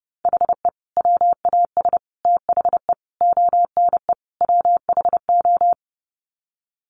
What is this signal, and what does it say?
Morse code, heard as a keyed tone, about 30 wpm, 700 Hz. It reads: HE WAS THE ONE WHO
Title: Sets of 5 Words Encore2 30wpm